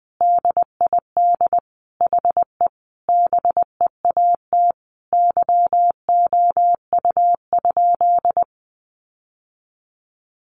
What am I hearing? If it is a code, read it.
Morse code, heard as a keyed tone, about 20 wpm, 700 Hz. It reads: DID HE BEAT YOU?
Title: Sentences from Top 1300 Words 20wpm